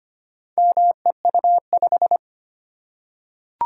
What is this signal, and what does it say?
Morse code, heard as a keyed tone, about 25 wpm, 700 Hz. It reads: MEU5
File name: practice / 25wpm / Four Letter-Number Combinations - 25wpm